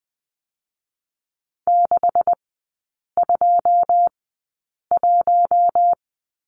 Morse code, heard as a keyed tone, about 20 wpm, spaced as for 8 wpm, 700 Hz. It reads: 621
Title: 127 Lesson 21e - Number 6 - Three Characters 20wpm